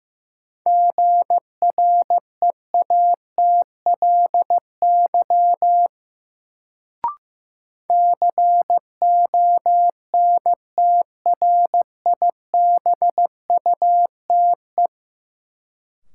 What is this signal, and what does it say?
Morse code, heard as a keyed tone, about 15 wpm, 700 Hz. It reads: GREATLY CONTRIBUTE
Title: Sentences from Top 1400 Words - New Word Drill Alt - 15wpm